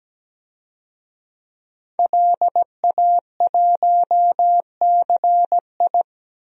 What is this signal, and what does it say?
Morse code, heard as a keyed tone, about 17 wpm, 700 Hz. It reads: LA1CI
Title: DXCC-100 Call Signs - 5 Characters - 17wpm